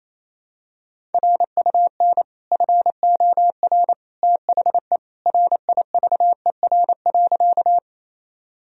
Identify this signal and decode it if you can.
Morse code, heard as a keyed tone, about 27 wpm, 700 Hz. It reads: RUN FOR THE RIVER.
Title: Sentences from Top 1000 Words - Review Part 1 - 28wpm